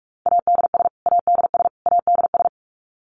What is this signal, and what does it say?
Morse code, heard as a keyed tone, about 44 wpm, 700 Hz. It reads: ADS ADS ADS
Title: Abbreviations Repeated 3x - 45wpm